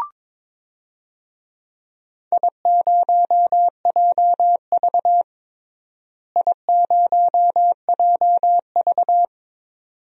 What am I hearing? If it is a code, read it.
Morse code, heard as a keyed tone, about 21 wpm, 700 Hz. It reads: I0JV I0JV
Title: DXCC-100 Call Signs - 4 Characters - Repeated 2x - 22wpm